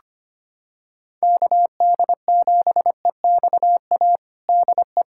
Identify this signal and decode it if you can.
Morse code, heard as a keyed tone, about 25 wpm, 700 Hz. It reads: KD7EXA DE
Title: USA Callsign DE Callsign - 6 Characters - 25wpm